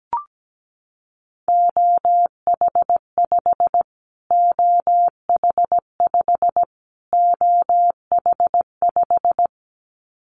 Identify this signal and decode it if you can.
Morse code, heard as a keyed tone, about 17 wpm, 700 Hz. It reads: OH5 OH5 OH5
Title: Callsign Prefixes for Top 25 DXCC - Repeated 3x - 17wpm